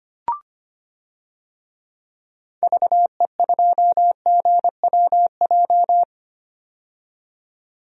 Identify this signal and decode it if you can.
Morse code, heard as a keyed tone, about 25 wpm, 700 Hz. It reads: VE2GWJ
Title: DXCC-100 Call Signs - 6 Characters - 25wpm